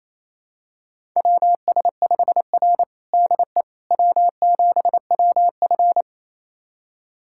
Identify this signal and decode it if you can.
Morse code, heard as a keyed tone, about 28 wpm, 700 Hz. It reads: WS5R DE W7WF
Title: USA Callsign DE Callsign - 4 Characters - 28wpm